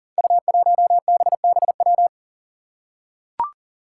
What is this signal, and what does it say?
Morse code, heard as a keyed tone, about 38 wpm, 700 Hz. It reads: U1BBW
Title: Five Challenging Letter-Number Combinations 40wpm